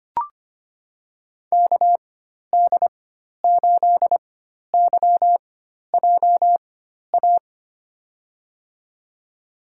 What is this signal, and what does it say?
Morse code, heard as a keyed tone, about 25 wpm, spaced as for 12 wpm, 700 Hz. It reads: KD8YJA
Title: USA Call Signs - 6 Characters - 4x Character Spacing - 25wpm